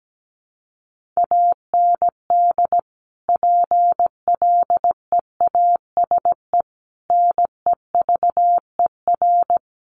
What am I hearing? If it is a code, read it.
Morse code, heard as a keyed tone, about 17 wpm, 700 Hz. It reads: AND PLEASE NEVER
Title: Sentences from Top 1500 Words 17wpm